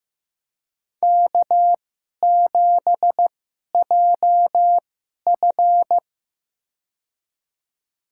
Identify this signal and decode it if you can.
Morse code, heard as a keyed tone, about 15 wpm, spaced as for 11 wpm, 700 Hz. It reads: K7JF
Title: USA Call Signs - 4 Characters - 2x Character Spacing - 15wpm